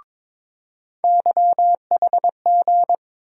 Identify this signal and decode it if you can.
Morse code, heard as a keyed tone, about 22 wpm, 700 Hz. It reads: YHG
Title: USA Call Sign Postfixes - 3 Letters 22wpm